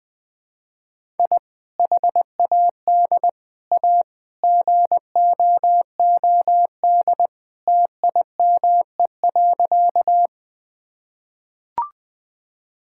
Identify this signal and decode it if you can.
Morse code, heard as a keyed tone, about 20 wpm, 700 Hz. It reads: I HAD A GOOD TIME.
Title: Sentences with 3-Word Phrases - 20wpm